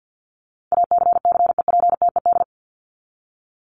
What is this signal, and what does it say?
Morse code, heard as a keyed tone, about 49 wpm, 700 Hz. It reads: ACCEPTED
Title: ED Suffix - Top 100 Words with Base Words 50wpm